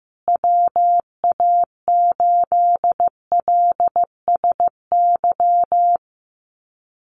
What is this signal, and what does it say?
Morse code, heard as a keyed tone, about 15 wpm, 700 Hz. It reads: WA8LSY
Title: USA Call Signs Repeated 3x 15wpm